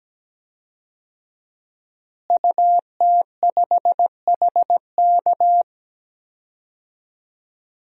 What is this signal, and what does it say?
Morse code, heard as a keyed tone, about 17 wpm, 700 Hz. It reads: UT5HK
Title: DXCC-100 Call Signs - 5 Characters - Repeated 2x - 17wpm